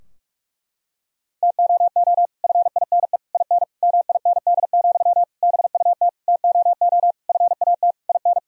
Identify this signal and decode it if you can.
Morse code, heard as a keyed tone, about 46 wpm, 700 Hz. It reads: TOO FINE IN MIND, BUT TOO FAT IN
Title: Sentences from Top 1000 Words - Review Part 4 - 45wpm